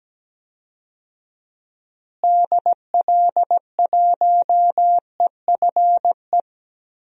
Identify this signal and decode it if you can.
Morse code, heard as a keyed tone, about 17 wpm, 700 Hz. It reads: DL1EFE
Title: DXCC-100 Call Signs - 6 Characters - Repeated 2x - 17wpm